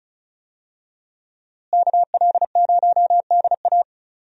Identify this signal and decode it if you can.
Morse code, heard as a keyed tone, about 34 wpm, 700 Hz. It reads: KL0DA
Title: USA Call Signs - 5 Characters - Repeated 2x - 35wpm